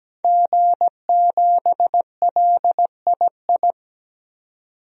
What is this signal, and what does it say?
Morse code, heard as a keyed tone, about 17 wpm, 700 Hz. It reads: G7LII
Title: DXCC-100 Call Signs - 5 Characters - 17wpm